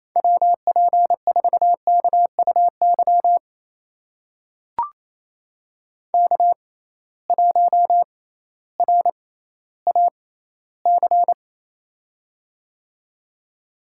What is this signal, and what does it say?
Morse code, heard as a keyed tone, about 28 wpm, spaced as for 15 wpm, 700 Hz. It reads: WP4KUY K1RAC
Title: USA Call Signs - 6x Character Spacing - 28wpm